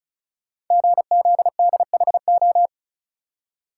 Morse code, heard as a keyed tone, about 36 wpm, 700 Hz. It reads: GZDHO
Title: Code Groups - 5 Letters - 35wpm